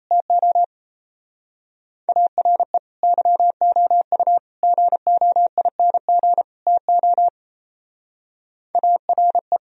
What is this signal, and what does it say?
Morse code, heard as a keyed tone, about 34 wpm, 700 Hz. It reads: TO ARE YOU GOING TO ARE
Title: Speed-Racing - Sets of 4 Words 28wpm